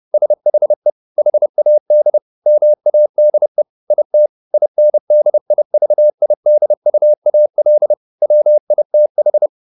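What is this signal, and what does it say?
Morse code, heard as a keyed tone, about 30 wpm, 600 Hz. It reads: SHE HAD MADE IT INDIVIDUAL WITH
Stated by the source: Morse code at 30 wpm, the novel Never by Ken Follett in English, from CW med Gnister.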